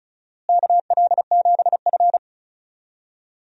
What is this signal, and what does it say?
Morse code, heard as a keyed tone, about 34 wpm, 700 Hz. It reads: KL7F
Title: DXCC-100 Call Signs - 4 Characters - Repeated 3x - 35wpm